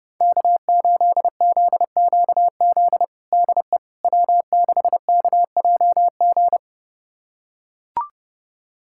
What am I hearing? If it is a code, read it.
Morse code, heard as a keyed tone, about 30 wpm, 700 Hz. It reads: K8ZQZ DE W6KJG
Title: USA Callsign DE Callsign - 5 Characters - 30wpm